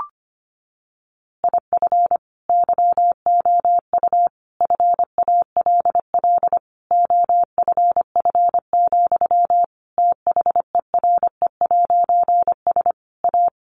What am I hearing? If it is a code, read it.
Morse code, heard as a keyed tone, about 25 wpm, 700 Hz. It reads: IF YOU FALL OFF, THERE'S A
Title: A Sound Of Thunder 25wpm